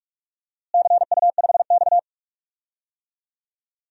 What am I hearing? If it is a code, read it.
Morse code, heard as a keyed tone, about 45 wpm, 700 Hz. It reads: CU5X
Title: DXCC-100 Call Signs - 4 Characters - Repeated 2x - 45wpm